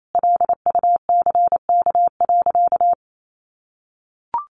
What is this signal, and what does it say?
Morse code, heard as a keyed tone, about 28 wpm, 700 Hz. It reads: LUCK.
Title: Sentences from Top 1500 Words 28wpm